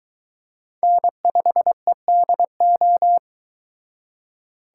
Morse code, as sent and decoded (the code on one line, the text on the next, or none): -. ..... -.. ---
N5EDO